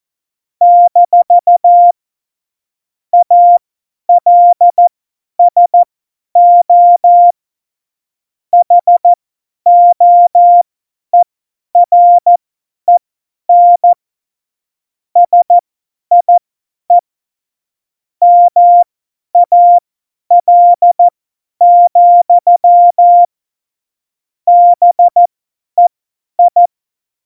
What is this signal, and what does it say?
Morse code, heard as a keyed tone, about 14 wpm, spaced as for 10 wpm, 700 Hz. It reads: - ALSO HOEREN SIE MAL, BEI
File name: Morsekurs / Witze / Witz 13. August 2019, 50 F70